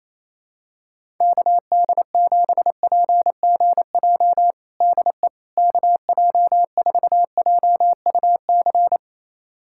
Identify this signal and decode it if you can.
Morse code, heard as a keyed tone, about 28 wpm, 700 Hz. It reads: KD7PGJ DE KJ4JUC
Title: USA Callsign DE Callsign 28wpm